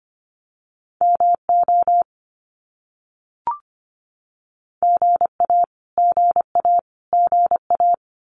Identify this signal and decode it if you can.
Morse code, heard as a keyed tone, about 25 wpm, 700 Hz. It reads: MO GA GA GA
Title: US States Repeated 3x 25wpm